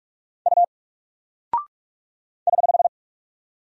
Morse code, heard as a keyed tone, about 48 wpm, 700 Hz. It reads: U <HH>
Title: Single Character (Letter:Number:Punctuation:Prosign) Alt 45wpm